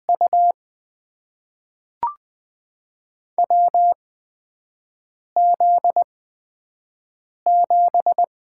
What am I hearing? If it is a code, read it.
Morse code, heard as a keyed tone, about 20 wpm, spaced as for 5 wpm, 700 Hz. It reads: U WZ7
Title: USA Call Signs - 4 Characters - 8x Character Spacing - 20wpm